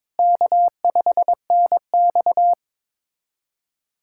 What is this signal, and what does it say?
Morse code, heard as a keyed tone, about 22 wpm, 700 Hz. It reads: K5NX
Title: USA Callsign DE Callsign - 4 Characters - 22wpm